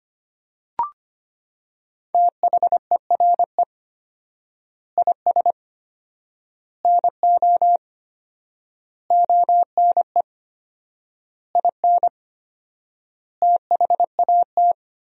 Morse code, heard as a keyed tone, about 25 wpm, 700 Hz. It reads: THERE IS NO ONE IN THAT
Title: 4x Word-Spacing - Sentences from Top 500 Words - 25wpm